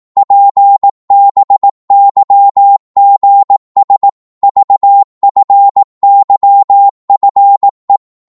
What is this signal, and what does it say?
Morse code, heard as a keyed tone, about 18 wpm, 800 Hz. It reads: PBYGS VFYFE
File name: Morse / ltr-18wpm